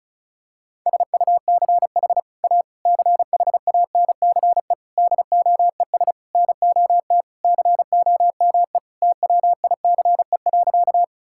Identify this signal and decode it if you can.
Morse code, heard as a keyed tone, about 36 wpm, 700 Hz. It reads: SUCH A CHANCE DOES NOT COME TWICE.
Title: Sentences from Top 1600 Words 35wpm